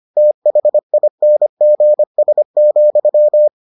Morse code, heard as a keyed tone, about 25 wpm, 600 Hz. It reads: THINGS,